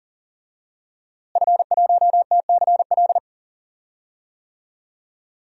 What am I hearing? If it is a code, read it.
Morse code, heard as a keyed tone, about 38 wpm, 700 Hz. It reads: F1TCL
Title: DXCC-100 Call Signs - 5 Characters - Repeated 3x - 40wpm